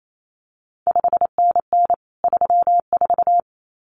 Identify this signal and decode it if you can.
Morse code, heard as a keyed tone, about 28 wpm, 700 Hz. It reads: 5NN 34
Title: CQ WW DX Contest Exchange Encore - 28wpm